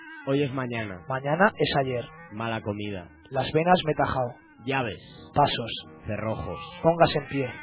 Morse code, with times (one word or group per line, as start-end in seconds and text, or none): none